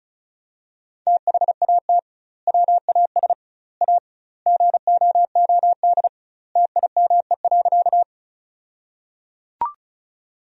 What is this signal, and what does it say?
Morse code, heard as a keyed tone, about 34 wpm, 700 Hz. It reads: THAT WAS A GOOD TIME.
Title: 2x Word-Spacing - Sentences from Top 200 Words - 35wpm